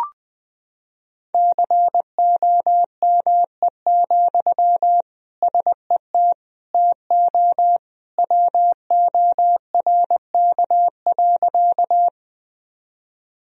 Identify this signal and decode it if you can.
Morse code, heard as a keyed tone, about 19 wpm, 700 Hz. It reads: COME, SET TO WORK.
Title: Sentences from Top 1000 Words - Review Part 1 - 20wpm